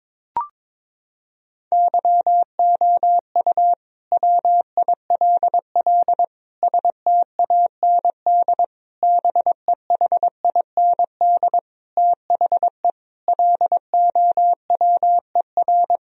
Morse code, heard as a keyed tone, about 22 wpm, 700 Hz. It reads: YOU WILL STAND BEHIND THE LOWER
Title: Sentences from Top 1000 Words - Review Part 4 - 22wpm